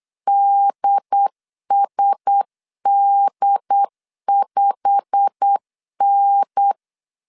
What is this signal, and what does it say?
Morse code, heard as a keyed tone, about 8 wpm, 800 Hz. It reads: DSD5N
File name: CW / DET / 1 9 1500C S C 8wpm 5653